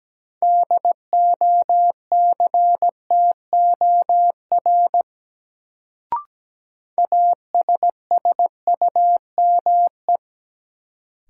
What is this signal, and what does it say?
Morse code, heard as a keyed tone, about 17 wpm, 700 Hz. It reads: DOCTOR ASSUME
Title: Sentences from Top 1500 Words - New Word Drill - 17wpm